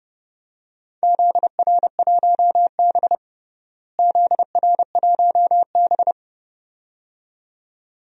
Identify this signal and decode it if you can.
Morse code, heard as a keyed tone, about 32 wpm, 700 Hz. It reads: ZR1B ZR1B
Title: DXCC-100 Call Signs - 4 Characters - Repeated 2x - 30wpm